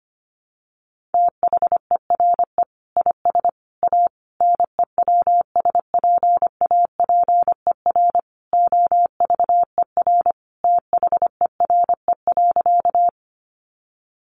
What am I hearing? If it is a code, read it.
Morse code, heard as a keyed tone, about 25 wpm, 690 Hz. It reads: THERE IS A NEWSPAPER OVER THERE.
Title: Sentences from Top 1100 Words 25wpm